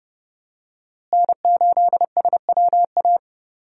Morse code, heard as a keyed tone, about 29 wpm, 700 Hz. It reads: N8SWA
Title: USA Call Signs - 5 Characters - Repeated 2x - 30wpm